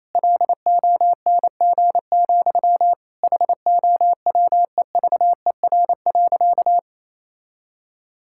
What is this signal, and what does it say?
Morse code, heard as a keyed tone, about 28 wpm, 700 Hz. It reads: LONG, HOWEVER.
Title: Sentences from Top 1000 Words - Review Part 6 - 28wpm